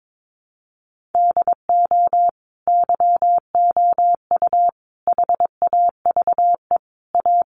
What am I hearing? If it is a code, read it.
Morse code, heard as a keyed tone, about 22 wpm, 700 Hz. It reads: DO YOU HAVE A
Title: Sentences from Top 1400 Words 22wpm